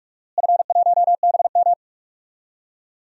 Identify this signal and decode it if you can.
Morse code, heard as a keyed tone, about 41 wpm, 700 Hz. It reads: F1BM
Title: DXCC-100 Call Signs - 4 Characters - Repeated 3x - 45wpm